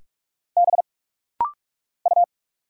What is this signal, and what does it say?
Morse code, heard as a keyed tone, about 46 wpm, 700 Hz. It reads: B U
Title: Single Letter-Number Alt - Rapid-Fire 45wpm